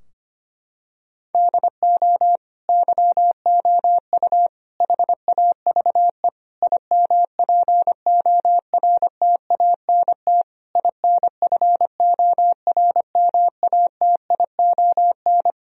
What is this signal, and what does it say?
Morse code, heard as a keyed tone, about 25 wpm, 700 Hz. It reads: DO YOU HAVE IMPORTANT INFORMATION
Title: Sentences from Top 1000 Words - Review Part 3 - 25wpm